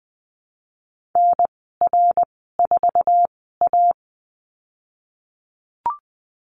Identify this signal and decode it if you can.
Morse code, heard as a keyed tone, about 19 wpm, spaced as for 14 wpm, 700 Hz. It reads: NR4A
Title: 055 Lesson 10g - Letter r - Call Signs 20wpm